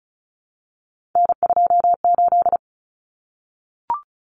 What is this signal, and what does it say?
Morse code, heard as a keyed tone, about 34 wpm, 700 Hz. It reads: N28
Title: Common Cut Numbers - 3-Digit 35wpm